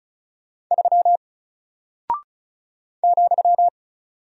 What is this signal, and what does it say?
Morse code, heard as a keyed tone, about 36 wpm, 700 Hz. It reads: 3 ,
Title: Single Character (Letter:Number:Punctuation:Prosign) Alt 35wpm